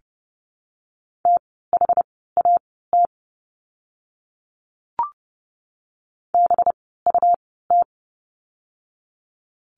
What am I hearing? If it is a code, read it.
Morse code, heard as a keyed tone, about 30 wpm, spaced as for 17 wpm, 700 Hz. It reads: THAT BUT
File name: course / 30:10wpm / 237 Lesson 38f - Letter q - Words 30wpm